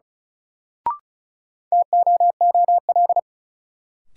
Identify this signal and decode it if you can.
Morse code, heard as a keyed tone, about 34 wpm, 700 Hz. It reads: TOOL
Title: Sentences from Top 1000 Words - Review Part 6 - New Word Drill - 35wpm